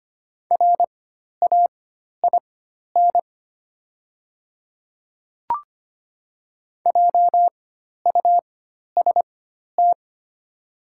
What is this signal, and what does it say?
Morse code, heard as a keyed tone, about 25 wpm, spaced as for 12 wpm, 700 Hz. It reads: RAIN JUST
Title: 4x Character Spacing - Top 300 Words - 25wpm